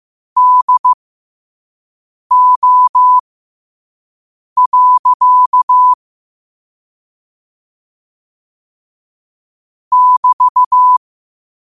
Morse code, heard as a keyed tone, about 15 wpm, spaced as for 5 wpm, 1000 Hz. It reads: DO. <BT>
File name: qotdfiles / 5wpm